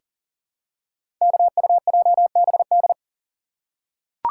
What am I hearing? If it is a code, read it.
Morse code, heard as a keyed tone, about 39 wpm, 700 Hz. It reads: KUJBD
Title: Code Groups - 5 Letter:Numbers - 40wpm